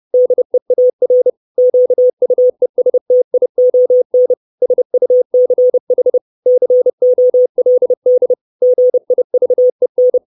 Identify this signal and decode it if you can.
Morse code, heard as a keyed tone, about 30 wpm, 500 Hz. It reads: DEAR QUESTION SUCH COLD GIVEN
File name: 632 Words 30WPM